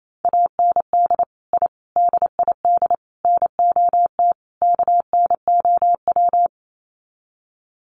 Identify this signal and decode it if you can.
Morse code, heard as a keyed tone, about 28 wpm, 700 Hz. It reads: AND I DID NOT KNOW
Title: Sets of 5 Words Encore 28wpm